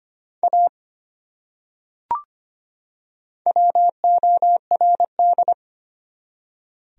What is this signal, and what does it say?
Morse code, heard as a keyed tone, about 25 wpm, 700 Hz. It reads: A WORD
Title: Top 100 Words 25wpm